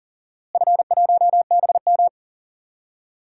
Morse code, heard as a keyed tone, about 40 wpm, 700 Hz. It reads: F1BM
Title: DXCC-100 Call Signs - 4 Characters - Repeated 2x - 40wpm